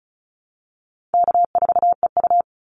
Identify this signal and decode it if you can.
Morse code, heard as a keyed tone, about 34 wpm, 700 Hz. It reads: K4EU
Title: Easy Call Signs Repeated 3x - 35wpm